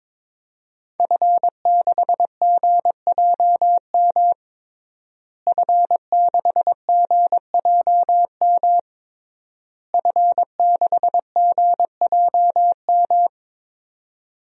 Morse code, heard as a keyed tone, about 22 wpm, 700 Hz. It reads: F6GJM F6GJM F6GJM
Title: DXCC-100 Call Signs - 5 Characters - Repeated 3x - 22wpm